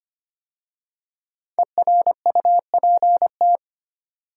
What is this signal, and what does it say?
Morse code, heard as a keyed tone, about 25 wpm, 700 Hz. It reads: ERUPT
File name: 5 Letter Words - First 3 Letters Given - 25wpm